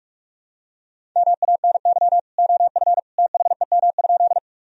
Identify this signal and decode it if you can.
Morse code, heard as a keyed tone, about 46 wpm, 700 Hz. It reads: MANY OF THEM?
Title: Sentences from Top 1000 Words - Review Part 1 - 45wpm